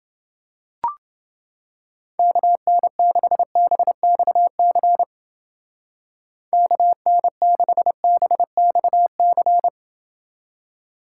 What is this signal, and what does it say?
Morse code, heard as a keyed tone, about 30 wpm, 700 Hz. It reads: KN6BXC KN6BXC